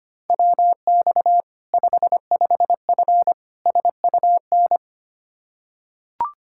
Morse code, heard as a keyed tone, about 25 wpm, 700 Hz. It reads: WX 55F SUN
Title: K1USN SST QSO Practice 25wpm